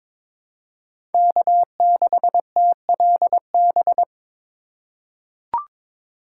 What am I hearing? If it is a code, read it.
Morse code, heard as a keyed tone, about 21 wpm, 700 Hz. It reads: K6TLB
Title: USA Call Signs - 4x Character Spacing - 22wpm